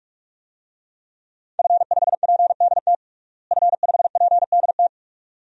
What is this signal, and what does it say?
Morse code, heard as a keyed tone, about 43 wpm, 700 Hz. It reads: F5PDT F5PDT
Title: DXCC-100 Call Signs - 5 Characters - Repeated 2x - 45wpm